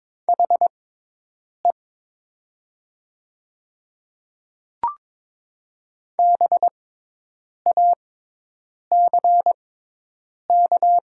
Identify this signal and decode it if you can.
Morse code, heard as a keyed tone, about 22 wpm, spaced as for 8 wpm, 700 Hz. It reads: HE BACK
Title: 6x Character Spacing - Top 200 Words - 22wpm